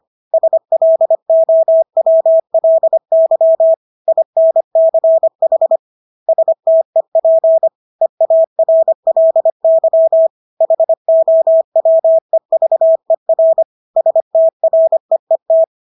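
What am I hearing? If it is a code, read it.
Morse code, heard as a keyed tone, about 25 wpm, 650 Hz. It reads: SLOWLY INCH STEP EARLY HOWEVER STREET
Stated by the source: Morse practice of English words at 25 wpm from CW med Gnister.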